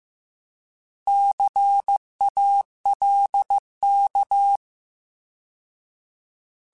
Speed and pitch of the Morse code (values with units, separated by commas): 15 wpm, 800 Hz